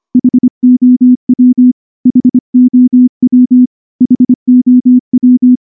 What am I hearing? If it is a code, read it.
Morse code, heard as a keyed tone, about 24 wpm, 250 Hz. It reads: HOW HOW HOW
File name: morse / morse-training-25WPM